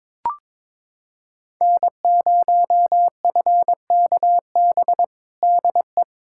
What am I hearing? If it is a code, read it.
Morse code, heard as a keyed tone, about 22 wpm, 700 Hz. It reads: N0FKB DE
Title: USA Callsign DE Callsign - 5 Characters - 22wpm